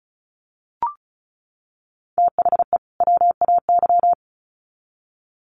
Morse code, heard as a keyed tone, about 34 wpm, 700 Hz. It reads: THE WAY
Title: Sets of 2 Words 35wpm